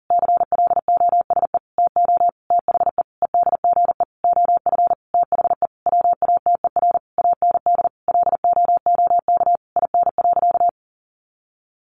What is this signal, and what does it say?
Morse code, heard as a keyed tone, about 40 wpm, 700 Hz. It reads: CLOSE TO THE EDGE OF THE WATER AND LOOK IN.
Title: Sentences from Top 1300 Words 40wpm